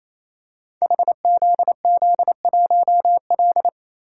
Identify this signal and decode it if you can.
Morse code, heard as a keyed tone, about 28 wpm, 700 Hz. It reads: HZZ1L
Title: Code Groups - 5 Letter:Numbers - 28wpm